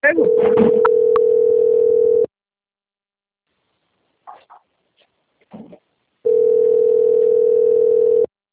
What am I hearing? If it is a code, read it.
Ringback tone (North America), 440 + 480 Hz, with speech.
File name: out- - - - - .4189